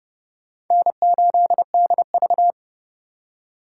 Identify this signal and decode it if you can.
Morse code, heard as a keyed tone, about 37 wpm, 700 Hz. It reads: N8DV
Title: Speed Racing - USA 1x2 Call Signs 30wpm